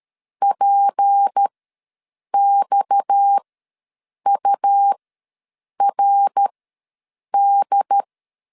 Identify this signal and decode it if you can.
Morse code, heard as a keyed tone, about 13 wpm, spaced as for 7 wpm, 800 Hz. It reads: PXURD
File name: LEZIONI / 6 PKX0 / 6 PKX ws 12wpm 3106